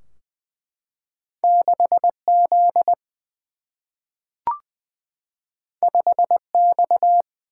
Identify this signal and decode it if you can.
Morse code, heard as a keyed tone, about 20 wpm, 700 Hz. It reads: 6 Z 5 X
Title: Single-Letter Single-Number 20wpm